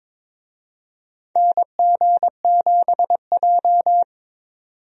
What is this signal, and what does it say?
Morse code, heard as a keyed tone, about 22 wpm, 700 Hz. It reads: NG7J
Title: USA Call Signs - 4 Characters - 22wpm